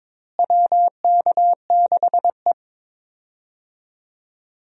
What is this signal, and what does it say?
Morse code, heard as a keyed tone, about 22 wpm, 700 Hz. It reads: WK6E
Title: USA Call Signs - 4 Characters - Repeated 3x - 22wpm